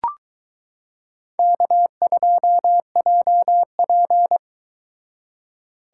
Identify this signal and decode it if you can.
Morse code, heard as a keyed tone, about 22 wpm, 700 Hz. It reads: K2JP